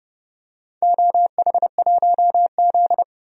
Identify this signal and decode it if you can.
Morse code, heard as a keyed tone, about 28 wpm, 700 Hz. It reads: OH1Z
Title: DXCC-100 Call Signs - 4 Characters - 30wpm